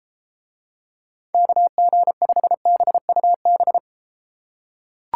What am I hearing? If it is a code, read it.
Morse code, heard as a keyed tone, about 32 wpm, 700 Hz. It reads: KG5BUB